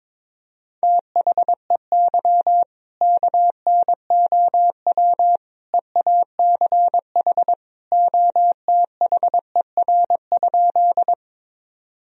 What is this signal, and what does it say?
Morse code, heard as a keyed tone, about 22 wpm, 700 Hz. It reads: THEY KNOW EACH OTHER?
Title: Sentences from Top 1000 Words - Review Part 1 - 22wpm